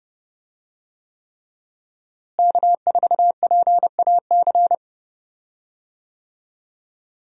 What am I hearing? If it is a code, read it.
Morse code, heard as a keyed tone, about 30 wpm, 700 Hz. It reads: K4PAC
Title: USA Call Signs - 5 Characters - Repeated 3x - 30wpm